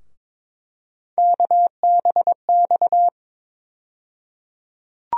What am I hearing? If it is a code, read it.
Morse code, heard as a keyed tone, about 22 wpm, 700 Hz. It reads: KBX
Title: USA Call Sign Postfixes - 3 Letters 22wpm